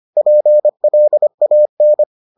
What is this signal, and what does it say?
Morse code, heard as a keyed tone, about 25 wpm, 600 Hz. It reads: PLAN